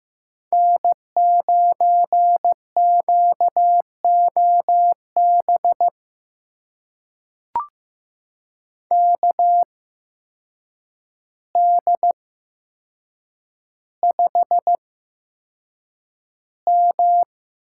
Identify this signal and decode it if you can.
Morse code, heard as a keyed tone, about 15 wpm, spaced as for 7 wpm, 700 Hz. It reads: N9QOB KD5M
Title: USA Call Signs - 5 Characters - 8x Character Spacing - 15wpm